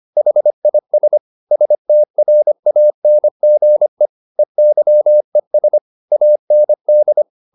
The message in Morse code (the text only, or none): HIS STRANGE EYES AND